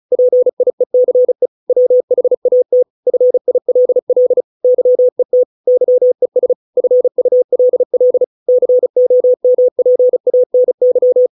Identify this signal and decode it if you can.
Morse code, heard as a keyed tone, about 34 wpm, 500 Hz. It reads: PIECE WHAT FILL YET YES FULL COMPANY